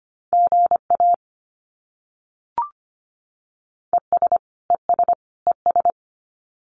Morse code, heard as a keyed tone, about 25 wpm, 700 Hz. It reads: GA ES ES ES
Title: Abbreviations Repeated 3x - 25wpm